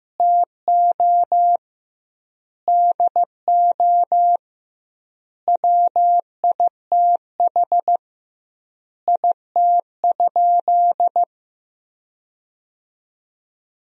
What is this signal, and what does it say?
Morse code, heard as a keyed tone, about 16 wpm, 700 Hz. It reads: TO DO WITH IT?
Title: 2x Word-Spacing - Sentences from Top 100 Words - 15wpm